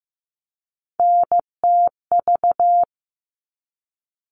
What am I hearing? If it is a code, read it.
Morse code, heard as a keyed tone, about 15 wpm, 700 Hz. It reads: NTV
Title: All Cut Numbers - 3-Digit 15wpm